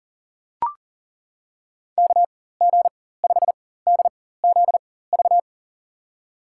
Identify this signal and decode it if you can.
Morse code, heard as a keyed tone, about 40 wpm, spaced as for 19 wpm, 700 Hz. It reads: KG5DZV
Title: USA Call Signs - 6 Characters - 4x Character Spacing - 40wpm